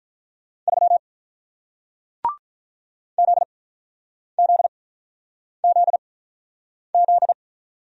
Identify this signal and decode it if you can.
Morse code, heard as a keyed tone, about 47 wpm, spaced as for 9 wpm, 700 Hz. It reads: 3 ZZZZ